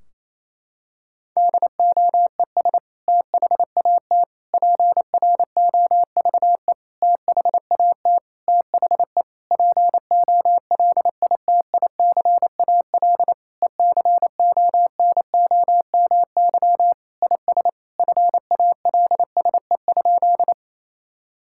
Morse code, heard as a keyed tone, about 28 wpm, 700 Hz. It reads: DOES THAT PROVE THAT THE POLITICAL ECONOMY IS FALSE?
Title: Sentences from Top 1600 Words 28wpm